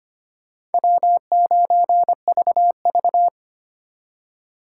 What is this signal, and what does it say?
Morse code, heard as a keyed tone, about 32 wpm, 700 Hz. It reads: W9VV